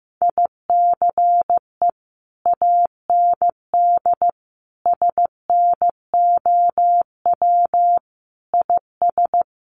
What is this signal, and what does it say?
Morse code, heard as a keyed tone, about 15 wpm, 700 Hz. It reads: ICE AND SNOW IS
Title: Sentences from Top 1000 Words 15wpm